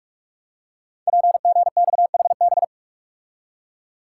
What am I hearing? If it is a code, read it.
Morse code, heard as a keyed tone, about 44 wpm, 700 Hz. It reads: PGXHB
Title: Code Groups - 5 Letters - 45wpm